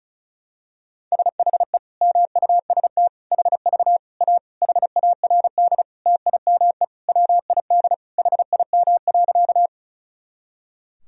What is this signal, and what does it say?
Morse code, heard as a keyed tone, about 36 wpm, 700 Hz. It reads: SHE MUST HV A HARD TIME WID HIM.
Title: Sentences from Top 300 Words and Abbreviations 35wpm